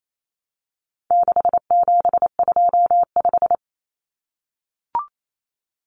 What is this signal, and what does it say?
Morse code, heard as a keyed tone, about 28 wpm, 700 Hz. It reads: 6725
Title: Numbers - 4-Digit 28wpm